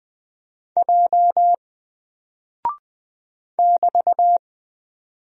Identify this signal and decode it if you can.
Morse code, heard as a keyed tone, about 20 wpm, 700 Hz. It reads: J <BT>
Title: Single Character (Letter:Number:Punctuation:Prosign) Alt 20wpm